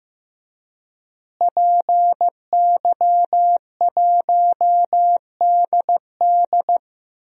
Morse code, heard as a keyed tone, about 15 wpm, 700 Hz. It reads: PY1DD
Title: DXCC-100 Call Signs - 5 Characters - Repeated 3x - 15wpm